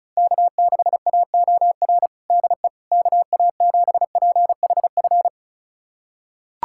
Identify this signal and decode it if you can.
Morse code, heard as a keyed tone, about 34 wpm, 700 Hz. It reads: K6AOR DE KA7PHF
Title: USA Callsign DE Callsign 35wpm